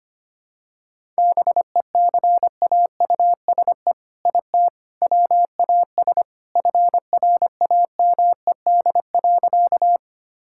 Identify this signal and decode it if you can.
Morse code, heard as a keyed tone, about 25 wpm, 700 Hz. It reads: BECAUSE IT WAS FRAMED.
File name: practice / 25wpm / All-Age Appropriate Jokes - 25wpm